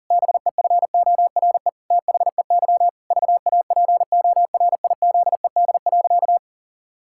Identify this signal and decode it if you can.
Morse code, heard as a keyed tone, about 40 wpm, 700 Hz. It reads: BEFORE THEY VAPORIZED.
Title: Pangram Challenge v2 - 40wpm